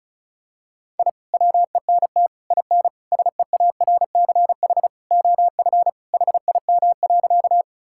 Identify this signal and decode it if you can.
Morse code, heard as a keyed tone, about 36 wpm, 700 Hz. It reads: I WENT IN SEARCH OF HIM.
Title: Sentences from Top 1000 Words - Review Part 5 - 35wpm